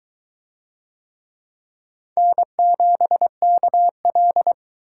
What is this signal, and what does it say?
Morse code, heard as a keyed tone, about 20 wpm, 700 Hz. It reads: N7KL